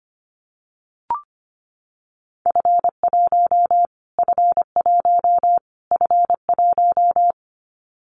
Morse code, heard as a keyed tone, about 25 wpm, 700 Hz. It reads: F1 F1 F1
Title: Callsign Prefixes for Top 25 DXCC - Repeated 3x - 25wpm